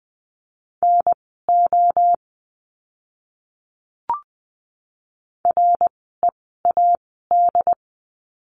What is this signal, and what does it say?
Morse code, heard as a keyed tone, about 20 wpm, spaced as for 15 wpm, 700 Hz. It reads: NO READ
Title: 094 Lesson 16f - Letter c - Words 20wpm